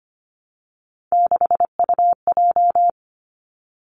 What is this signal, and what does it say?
Morse code, heard as a keyed tone, about 25 wpm, 700 Hz. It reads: 6UJ
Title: Three Challenging Letter-Number Combinations 25wpm